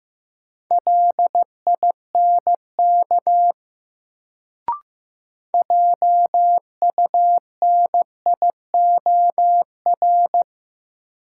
Morse code, heard as a keyed tone, about 15 wpm, 700 Hz. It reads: LINK JUNIOR
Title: Sentences from Top 1000 Words - Review Part 7 - New Word Drill - 15wpm